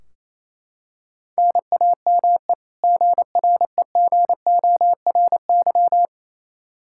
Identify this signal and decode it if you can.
Morse code, heard as a keyed tone, about 28 wpm, 700 Hz. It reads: NAME GREGORY
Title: Practice QSO - Exchange 1 - Beginning - 28wpm